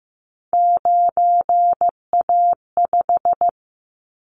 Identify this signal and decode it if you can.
Morse code, heard as a keyed tone, about 15 wpm, 700 Hz. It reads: 9A5
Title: Callsign Prefixes for Top 50 DXCC - 15wpm